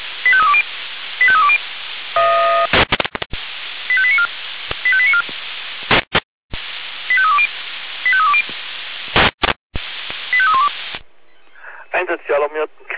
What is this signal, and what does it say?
Five-tone selective call (ZVEI), heard as quick runs of five tones, tones of about 70 ms, with speech. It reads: 85320 85320 85894 85894 85320 85320 85312